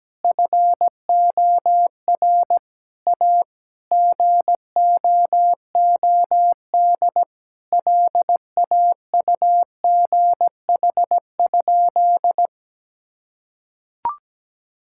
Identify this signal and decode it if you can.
Morse code, heard as a keyed tone, about 17 wpm, 700 Hz. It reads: FOR A GOOD LAUGH?
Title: Sentences from Top 1000 Words - Review Part 5 - 17wpm